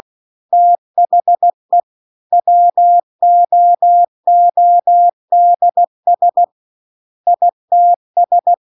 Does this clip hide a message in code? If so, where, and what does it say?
Yes: Morse code from 0.5 to 8.5 s, heard as a keyed tone, about 16 wpm, 700 Hz. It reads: THE WOODS ITS